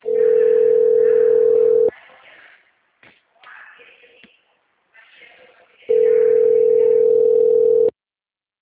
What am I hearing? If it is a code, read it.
Ringback tone (North America), 440 + 480 Hz.
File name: out- -2045- - - .5383